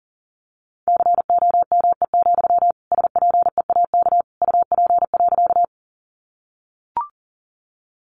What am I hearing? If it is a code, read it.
Morse code, heard as a keyed tone, about 40 wpm, 700 Hz. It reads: COME, SPEAK UP.
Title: Sentences from Top 1100 Words 40wpm